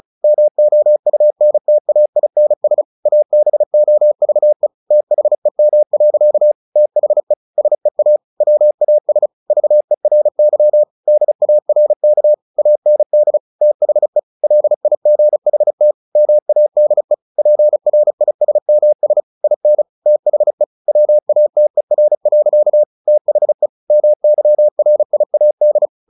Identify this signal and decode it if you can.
Morse code, heard as a keyed tone, about 34 wpm, 600 Hz. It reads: MOUNTAINS ABOVE THEM. THE SEA WAS VERY DARK AND THE LIGHT MADE PRISMS IN THE WATER. THE MYRIAD
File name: the old man and the sea / the old man and the sea 1 35wpm